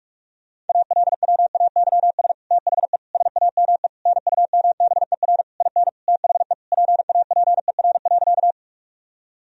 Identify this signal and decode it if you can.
Morse code, heard as a keyed tone, about 41 wpm, 700 Hz. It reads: ALWAYS THE SAME NUMBER IN THE PAPER.
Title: Sentences from Top 1000 Words - Review Part 1 - 45wpm